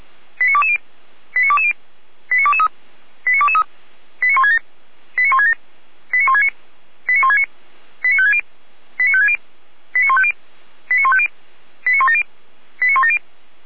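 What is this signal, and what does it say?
Five-tone selective call (ZVEI), heard as quick runs of five tones, tones of about 70 ms. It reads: 89220 89220 89223 89223 89167 89167 89169 89169 89560 89560 89150 89150 89170 89170
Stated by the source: Five-tone ZVEI selective call 89121 received by SDR.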